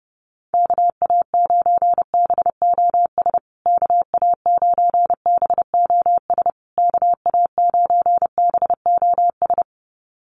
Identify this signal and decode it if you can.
Morse code, heard as a keyed tone, about 30 wpm, 700 Hz. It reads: KA9BOS KA9BOS KA9BOS
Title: USA Call Signs Repeated 3x 30wpm